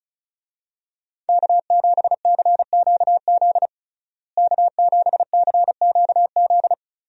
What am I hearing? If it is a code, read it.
Morse code, heard as a keyed tone, about 36 wpm, 700 Hz. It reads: K7CQZ K7CQZ
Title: USA Call Signs - 5 Characters - Repeated 3x - 35wpm